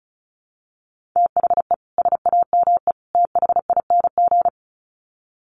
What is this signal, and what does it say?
Morse code, heard as a keyed tone, about 34 wpm, 700 Hz. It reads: THE SAME THING
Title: Sets of 3 Words 35wpm